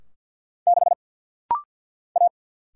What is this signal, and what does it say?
Morse code, heard as a keyed tone, about 50 wpm, 700 Hz. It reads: 6 A
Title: Single Letter-Number Alt - Rapid-Fire 50wpm